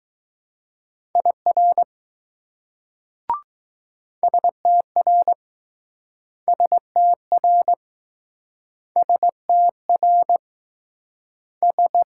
Morse code, heard as a keyed tone, about 19 wpm, 700 Hz. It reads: IR STR STR STR S